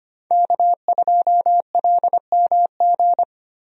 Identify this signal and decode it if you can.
Morse code, heard as a keyed tone, about 25 wpm, 700 Hz. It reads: K2LMG
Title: USA Call Signs - 5 Characters - Repeated 3x - 25wpm